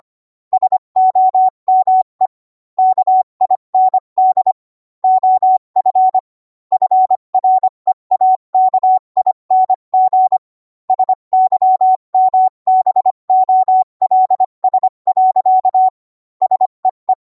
Morse code, heard as a keyed tone, about 25 wpm, 750 Hz. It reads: SOME KIND OF FREAKING SYMBOLS. SEE